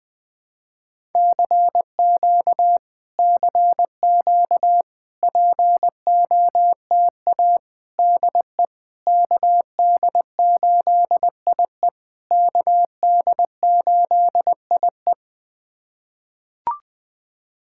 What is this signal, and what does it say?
Morse code, heard as a keyed tone, about 20 wpm, 700 Hz. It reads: CQ CQ POTA DE KD8IE KD8IE
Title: POTA QSO Practice - 20wpm